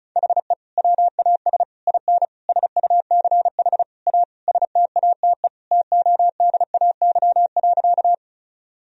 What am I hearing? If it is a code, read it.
Morse code, heard as a keyed tone, about 34 wpm, 700 Hz. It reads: HE WAS IN SUCH A STATE TODAY.
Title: Sentences from Top 1000 Words - Review Part 2 - 35wpm